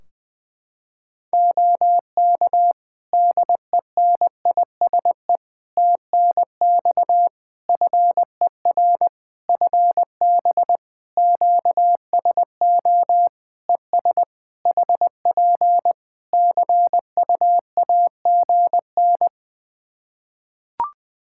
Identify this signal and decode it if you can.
Morse code, heard as a keyed tone, about 20 wpm, 700 Hz. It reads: OK DENISE TNX FER FB QSO ES HP CUAGN
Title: Practice QSO - Exchange 4 - Ending - 20wpm